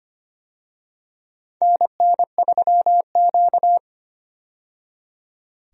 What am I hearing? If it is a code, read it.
Morse code, heard as a keyed tone, about 25 wpm, 700 Hz. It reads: NN3Q
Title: USA Call Signs - 4 Characters - 25wpm